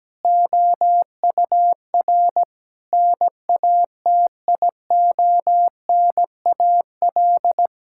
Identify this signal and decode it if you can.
Morse code, heard as a keyed tone, about 17 wpm, 700 Hz. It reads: OUR NATIONAL
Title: Sentences from Top 1000 Words - Review Part 5 - 17wpm